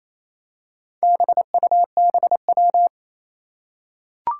Morse code, heard as a keyed tone, about 28 wpm, 700 Hz. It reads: BUBW
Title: Four Challenging Letter-Number Combinations 28wpm